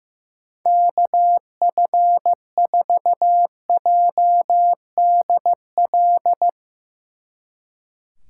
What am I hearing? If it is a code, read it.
Morse code, heard as a keyed tone, about 16 wpm, 700 Hz. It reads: KF4JDL
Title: Speed Racing - USA 2x3 Call Signs 15wpm